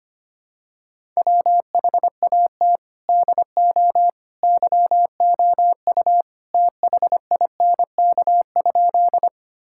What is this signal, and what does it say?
Morse code, heard as a keyed tone, about 25 wpm, 700 Hz. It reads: WHAT DO YOU THINK?
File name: practice / 25wpm / Sentences with 3-Word Phrases - 25wpm